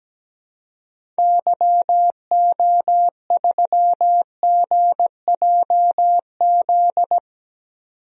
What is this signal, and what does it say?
Morse code, heard as a keyed tone, about 17 wpm, 700 Hz. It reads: YO3GJZ
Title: DXCC-100 Call Signs - 6 Characters - 17wpm